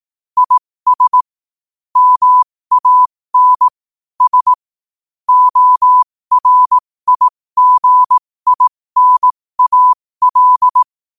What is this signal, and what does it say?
Morse code, heard as a keyed tone, about 18 wpm, spaced as for 15 wpm, 1000 Hz. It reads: IS MAN S ORIGINAL
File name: qotdfiles / 15wpm